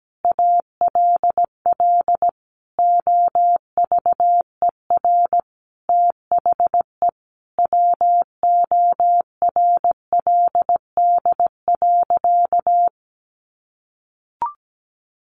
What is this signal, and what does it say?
Morse code, heard as a keyed tone, about 17 wpm, 700 Hz. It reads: ALL OVER THE WORLD.
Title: Additional Easy Phrases 17wpm